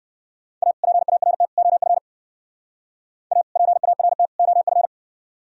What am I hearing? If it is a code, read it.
Morse code, heard as a keyed tone, about 58 wpm, 700 Hz. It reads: A PART OF A PART OF